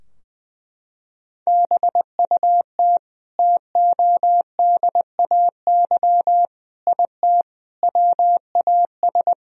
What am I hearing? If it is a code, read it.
Morse code, heard as a keyed tone, about 20 wpm, 700 Hz. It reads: BUT TODAY IT WAS
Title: Sentences from Top 1000 Words - Review Part 4 - 20wpm